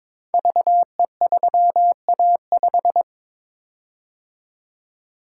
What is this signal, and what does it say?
Morse code, heard as a keyed tone, about 22 wpm, 700 Hz. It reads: VE3A5
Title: Code Groups - 5 Letter:Numbers - 22wpm